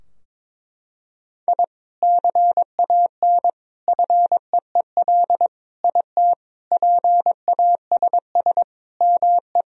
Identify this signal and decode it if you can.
Morse code, heard as a keyed tone, about 22 wpm, 700 Hz. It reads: I CAN FEEL IT PASS ME
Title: Sentences from Top 1000 Words - Review Part 2 - 22wpm